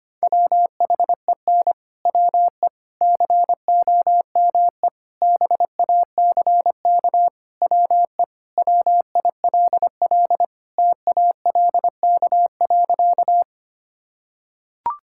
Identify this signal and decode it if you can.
Morse code, heard as a keyed tone, about 25 wpm, 700 Hz. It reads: WHEN WE COME BACK WE WILL TALK.
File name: practice / 25wpm / Sentences with 4-Word Phrases - 25wpm